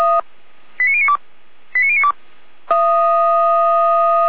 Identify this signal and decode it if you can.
Five-tone selective call (ZVEI), heard as quick runs of five tones, tones of about 70 ms. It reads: 89092 89092
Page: /455/SDR/ZVEI/89115/2021.07.11/